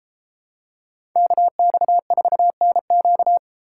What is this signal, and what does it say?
Morse code, heard as a keyed tone, about 30 wpm, 700 Hz. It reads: KX4NQ